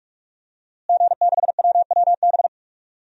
Morse code, heard as a keyed tone, about 46 wpm, 700 Hz. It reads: G6WWB
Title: DXCC-100 Call Signs - 5 Characters - Repeated 2x - 45wpm